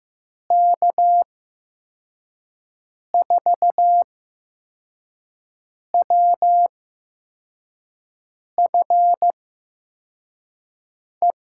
Morse code, heard as a keyed tone, about 15 wpm, spaced as for 4 wpm, 700 Hz. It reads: K4WFE
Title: USA Call Signs - 6 Characters - 8x Character Spacing - 15wpm